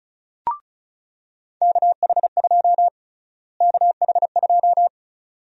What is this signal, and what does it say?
Morse code, heard as a keyed tone, about 36 wpm, 700 Hz. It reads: KH2 KH2
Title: USA Call Sign Prefixes - Repeated 2x - 35wpm